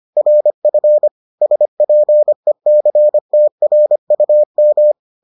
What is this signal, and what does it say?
Morse code, heard as a keyed tone, about 25 wpm, 600 Hz. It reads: RF SPECTRUM